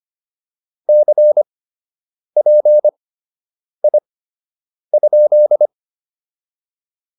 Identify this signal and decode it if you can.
Morse code, heard as a keyed tone, about 25 wpm, spaced as for 8 wpm, 600 Hz. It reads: CPI?